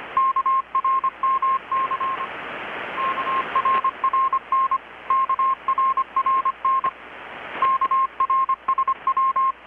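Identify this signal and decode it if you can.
Morse code, heard as a keyed tone, about 25 wpm, 1060 Hz. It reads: KRMC KRRN KRRN KRSW
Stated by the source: Morse from KPH's Night of Nights XX maritime radiotelegraphy event, received in Chiapas, Mexico about 3,800 km away on a Kenwood R-600 with a grounded 10 m long-wire antenna.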